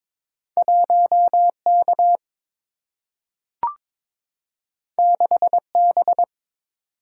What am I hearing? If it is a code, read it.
Morse code, heard as a keyed tone, about 22 wpm, 700 Hz. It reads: 1 K 6 B
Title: Single-Letter Single-Number 22wpm